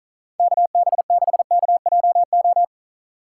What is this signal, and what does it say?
Morse code, heard as a keyed tone, about 43 wpm, 700 Hz. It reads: KB6KJO
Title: Speed Racing - USA 2x3 Call Signs 35wpm